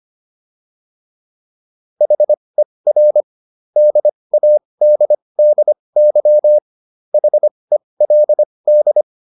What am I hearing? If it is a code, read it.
Morse code, heard as a keyed tone, about 25 wpm, spaced as for 20 wpm, 600 Hz. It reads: HER DADDY HELD
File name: SS123 20